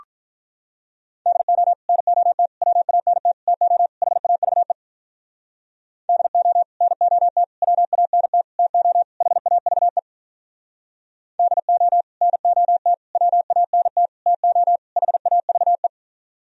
Morse code, heard as a keyed tone, about 46 wpm, 700 Hz. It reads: DO NOT WANT TO HAVE DO NOT WANT TO HAVE DO NOT WANT TO HAVE